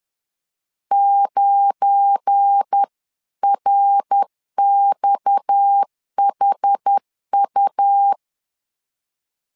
Morse code, heard as a keyed tone, about 10 wpm, 800 Hz. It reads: 9RXHU
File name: LEZIONI / 7 GF9Z / 7 GF9 10wpm 2747